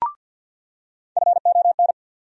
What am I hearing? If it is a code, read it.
Morse code, heard as a keyed tone, about 47 wpm, 700 Hz. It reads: FON
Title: USA Call Sign Postfixes - 3 Letters 50wpm